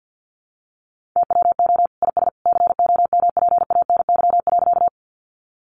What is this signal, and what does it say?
Morse code, heard as a keyed tone, about 50 wpm, 700 Hz. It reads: TWO IS COMPANY.
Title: Sentences from Top 600 Words 50wpm